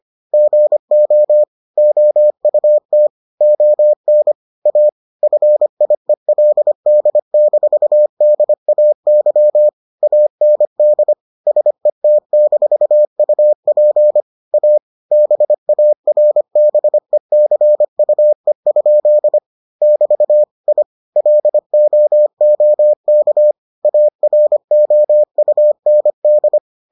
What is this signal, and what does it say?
Morse code, heard as a keyed tone, about 25 wpm, 600 Hz. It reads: GO OUT ON A FIELD-DAY AND SET-UP A BARBECUE? <BT> I LOOK AROUND